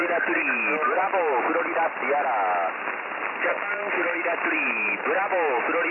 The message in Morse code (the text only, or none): none